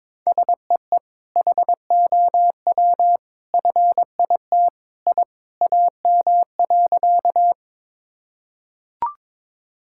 Morse code, heard as a keyed tone, about 22 wpm, 700 Hz. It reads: SEE HOW FIT I AM.
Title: Sentences from Top 1000 Words - Review Part 5 - 22wpm